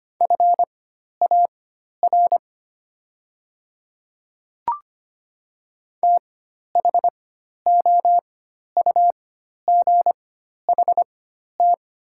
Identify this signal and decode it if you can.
Morse code, heard as a keyed tone, about 25 wpm, spaced as for 12 wpm, 690 Hz. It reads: FAR THOUGHT
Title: 4x Character Spacing - Top 500 Words - 25wpm